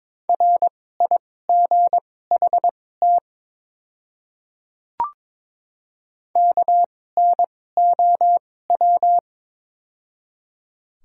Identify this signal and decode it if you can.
Morse code, heard as a keyed tone, about 22 wpm, spaced as for 16 wpm, 700 Hz. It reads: RIGHT KNOW
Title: 2x Character Spacing - Top 200 Words - 22wpm